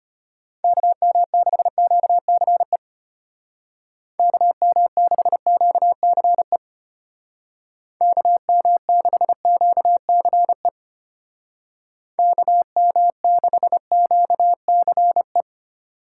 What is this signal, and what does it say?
Morse code, heard as a keyed tone, about 33 wpm, 700 Hz. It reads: KM6QCE KM6QCE KM6QCE KM6QCE